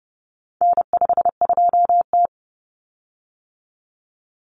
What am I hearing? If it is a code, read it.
Morse code, heard as a keyed tone, about 30 wpm, 700 Hz. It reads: N52T
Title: Common Cut Numbers - 4-Digit 30wpm